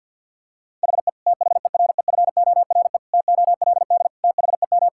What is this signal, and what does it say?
Morse code, heard as a keyed tone, about 50 wpm, 700 Hz. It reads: HE THEREFORE TOLD THEM
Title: Sentences from Top 1000 Words - Review Part 5 - 50wpm